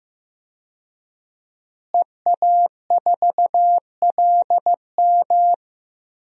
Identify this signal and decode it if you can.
Morse code, heard as a keyed tone, about 15 wpm, 700 Hz. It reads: EA4LM
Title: DXCC-100 Call Signs - 5 Characters - Repeated 3x - 15wpm